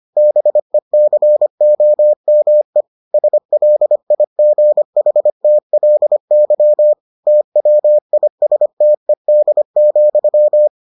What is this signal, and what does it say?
Morse code, heard as a keyed tone, about 25 wpm, 600 Hz. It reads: BECOME SLIGHTLY TWISTED,